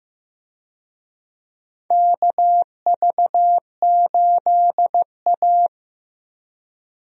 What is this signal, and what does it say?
Morse code, heard as a keyed tone, about 15 wpm, 700 Hz. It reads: KV8A
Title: USA Call Signs - 4 Characters - 15wpm